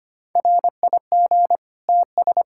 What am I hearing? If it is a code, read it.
Morse code, heard as a keyed tone, about 25 wpm, 700 Hz. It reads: RIG TS
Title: Practice QSO - Exchange 2 - Info - 25wpm